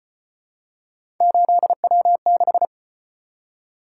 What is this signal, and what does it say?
Morse code, heard as a keyed tone, about 34 wpm, 700 Hz. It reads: 8W6